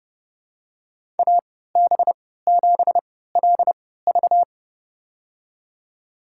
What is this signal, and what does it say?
Morse code, heard as a keyed tone, about 30 wpm, spaced as for 17 wpm, 700 Hz. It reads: AB7LV
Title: USA Call Signs - 5 Characters - 3x Character Spacing - 30wpm